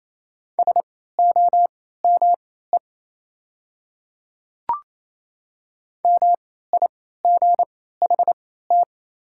Morse code, heard as a keyed tone, about 27 wpm, spaced as for 16 wpm, 690 Hz. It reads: SOME MIGHT
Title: 3x Character Spacing - Top 300 Words - 28wpm